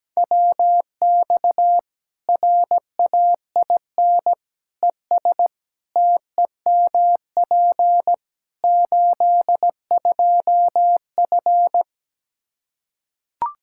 Morse code, heard as a keyed tone, about 17 wpm, 700 Hz. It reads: WX RAIN ES TEMP 82F
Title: Practice QSO - Exchange 2 - Info - 17wpm